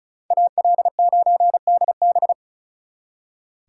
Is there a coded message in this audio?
Yes: Morse code from 0.3 to 2.3 s, heard as a keyed tone, about 44 wpm, 700 Hz. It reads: AL9DB